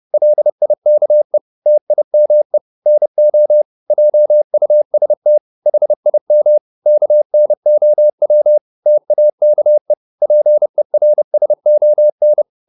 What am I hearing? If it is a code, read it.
Morse code, heard as a keyed tone, about 30 wpm, 600 Hz. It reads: LIKE TIME NO JUST HIM KNOW TAKE PERSON
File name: Top-100-Words-30